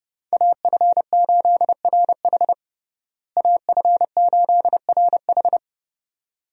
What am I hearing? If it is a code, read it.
Morse code, heard as a keyed tone, about 30 wpm, 700 Hz. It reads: AF8RH AF8RH
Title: USA Call Signs - 5 Characters - Repeated 2x - 30wpm